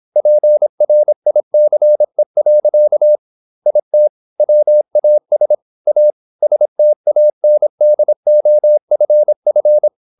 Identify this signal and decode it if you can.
Morse code, heard as a keyed tone, about 26 wpm, 600 Hz. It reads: PRICE. IT WAS A STANDOFF
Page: Never 26wpm | CW med Gnister